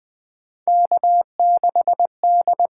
Morse code, heard as a keyed tone, about 20 wpm, 700 Hz. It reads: K6D
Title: USA Call Signs - 6x Character Spacing - 20wpm